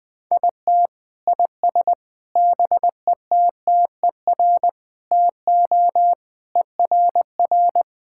Morse code, heard as a keyed tone, about 21 wpm, 700 Hz. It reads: IT IS BETTER TO ERR
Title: Sentences with 7-Word Phrases - Spoken Beginning - 20wpm